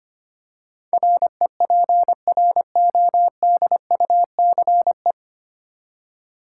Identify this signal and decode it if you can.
Morse code, heard as a keyed tone, about 25 wpm, 700 Hz. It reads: REPRODUCE
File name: Nine Letter Words - 25wpm